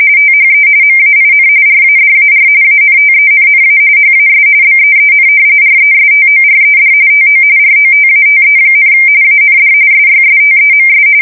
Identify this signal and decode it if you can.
RTTY, about 45.45 baud. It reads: THE MAIN RADAR ANTENNA HAS BEEN RESTORED TO ROTATE, AND SYSTEMS
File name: fsk170 / battleshipnj